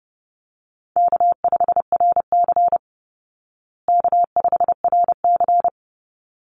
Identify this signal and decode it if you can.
Morse code, heard as a keyed tone, about 30 wpm, 700 Hz. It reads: K5RC K5RC
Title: Call Signs Repeated 3x - 30wpm